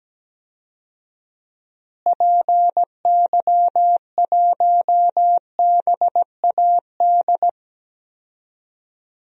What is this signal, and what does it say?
Morse code, heard as a keyed tone, about 17 wpm, 700 Hz. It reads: PY1BAD
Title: DXCC-100 Call Signs - 6 Characters - Repeated 3x - 17wpm